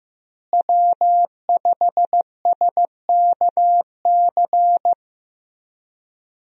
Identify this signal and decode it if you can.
Morse code, heard as a keyed tone, about 15 wpm, 700 Hz. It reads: W5SKC
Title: USA Call Signs - 5 Characters - Repeated 2x - 15wpm